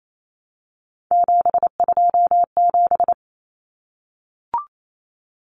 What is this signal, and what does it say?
Morse code, heard as a keyed tone, about 28 wpm, 700 Hz. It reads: 727
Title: Numbers - 3-Digit 28wpm